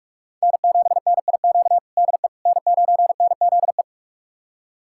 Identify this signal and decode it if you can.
Morse code, heard as a keyed tone, about 42 wpm, 700 Hz. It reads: N7NIQ DE N9NZE
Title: USA Callsign DE Callsign - 5 Characters - 45wpm